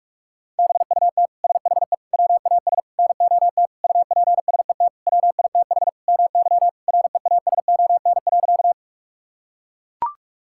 Morse code, heard as a keyed tone, about 42 wpm, 700 Hz. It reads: BUT SHE WAS NOT UPSET WITH MY REASON.
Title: Sentences from Top 1000 Words - Review Part 5 - 45wpm